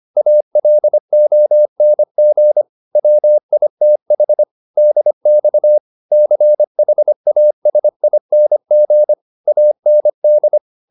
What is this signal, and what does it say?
Morse code, heard as a keyed tone, about 25 wpm, 600 Hz. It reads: ALONG WITH DX CHASING AND